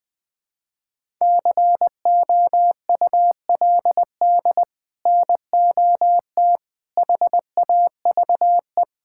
Morse code, heard as a keyed tone, about 20 wpm, 700 Hz. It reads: COULD NOT HAVE
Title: Sets of 8 Words - 20wpm